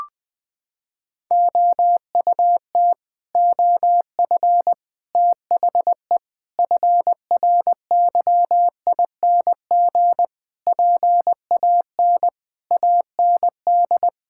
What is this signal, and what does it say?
Morse code, heard as a keyed tone, about 20 wpm, 700 Hz. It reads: OUT OF THE FRYING PAN AND
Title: Sets of 9 Words - 20wpm